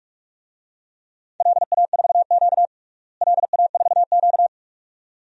Morse code, heard as a keyed tone, about 46 wpm, 700 Hz. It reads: LA4Q LA4Q
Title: DXCC-100 Call Signs - 4 Characters - Repeated 2x - 45wpm